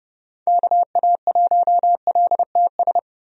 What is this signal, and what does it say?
Morse code, heard as a keyed tone, about 30 wpm, 700 Hz. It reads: KA1LTS
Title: USA Call Signs - 6 Characters - 30wpm